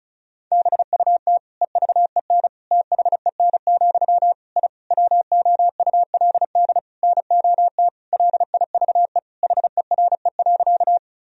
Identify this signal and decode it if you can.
Morse code, heard as a keyed tone, about 34 wpm, 700 Hz. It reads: BUT EVEN THEN, I WOULD NOT LIVE HERE.
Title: Sentences from Top 1000 Words - Review Part 1 - 35wpm